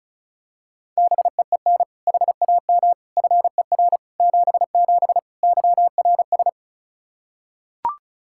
Morse code, heard as a keyed tone, about 36 wpm, 700 Hz. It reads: BEEN HAM FER 77 YRS
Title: Practice QSO - Exchange 3 - More Info - 35wpm